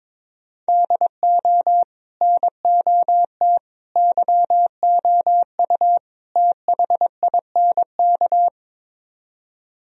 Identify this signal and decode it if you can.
Morse code, heard as a keyed tone, about 23 wpm, 700 Hz. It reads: DO NOT YOU THINK
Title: Speed-Racing - Sets of 4 Words 22wpm